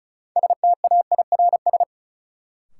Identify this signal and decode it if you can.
Morse code, heard as a keyed tone, about 36 wpm, 700 Hz. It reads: STAIRS
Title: Sets of 5 Words Encore - New Word Drill - 35wpm